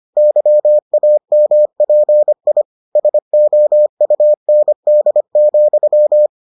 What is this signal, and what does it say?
Morse code, heard as a keyed tone, about 25 wpm, 600 Hz. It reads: YAMPI SOUND,